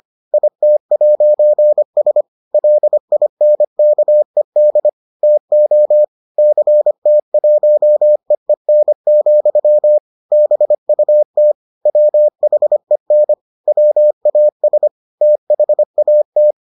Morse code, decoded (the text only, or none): IT'S LINKED TO CT1EEN, BUT WHEN WAS THAT